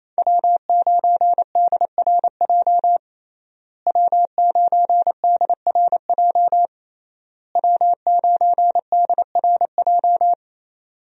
Morse code, heard as a keyed tone, about 27 wpm, 700 Hz. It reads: W9DRJ W9DRJ W9DRJ
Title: USA Call Signs - 5 Characters - Repeated 3x - 28wpm